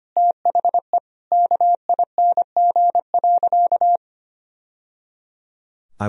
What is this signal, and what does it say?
Morse code, heard as a keyed tone, about 25 wpm, 700 Hz. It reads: THE KING.
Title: Sentences from Top 1000 Words - Review Part 3 - 25wpm